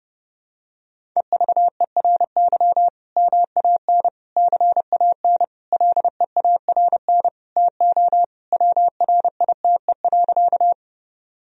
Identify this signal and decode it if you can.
Morse code, heard as a keyed tone, about 32 wpm, 700 Hz. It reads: EVERY MAN CAN LEARN TO WRITE.
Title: Sentences from Top 1000 Words - Review Part 1 - 30wpm